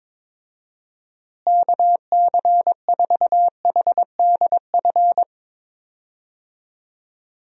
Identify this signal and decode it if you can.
Morse code, heard as a keyed tone, about 22 wpm, 700 Hz. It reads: KC4HDF
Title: USA Call Signs - 6 Characters - Repeated 3x - 22wpm